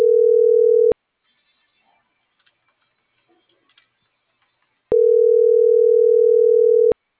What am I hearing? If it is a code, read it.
Ringback tone (North America), 440 + 480 Hz.